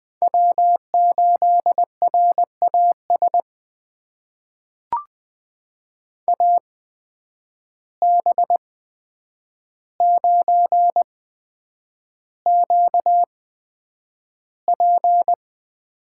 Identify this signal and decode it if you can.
Morse code, heard as a keyed tone, about 20 wpm, spaced as for 9 wpm, 700 Hz. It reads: W8RAS AB9QP
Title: USA Call Signs - 5 Characters - 8x Character Spacing - 20wpm